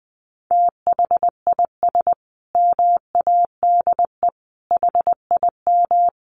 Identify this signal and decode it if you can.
Morse code, heard as a keyed tone, about 20 wpm, 700 Hz. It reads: THIS MADE HIM
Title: Sentences from Top 500 Words 20wpm